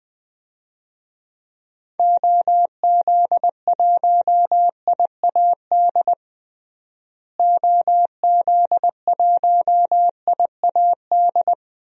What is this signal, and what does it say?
Morse code, heard as a keyed tone, about 20 wpm, 700 Hz. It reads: OZ1IAD OZ1IAD
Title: DXCC-100 Call Signs - 6 Characters - Repeated 2x - 20wpm